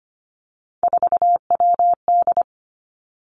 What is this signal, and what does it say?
Morse code, heard as a keyed tone, about 25 wpm, 700 Hz. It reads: 4WD
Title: Three Challenging Letter-Number Combinations 25wpm